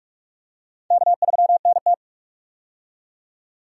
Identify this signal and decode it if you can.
Morse code, heard as a keyed tone, about 46 wpm, 700 Hz. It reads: K3NT
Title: USA Call Signs - 4 Characters - 45wpm